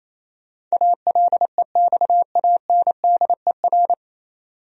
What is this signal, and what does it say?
Morse code, heard as a keyed tone, about 27 wpm, 700 Hz. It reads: ALEXANDER
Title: USA Common Names - 28wpm